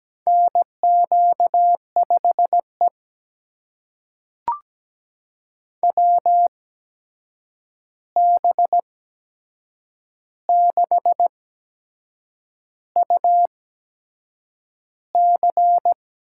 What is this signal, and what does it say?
Morse code, heard as a keyed tone, about 17 wpm, spaced as for 7 wpm, 700 Hz. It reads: NQ5E WB6UC
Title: USA Call Signs - 8x Character Spacing - 17wpm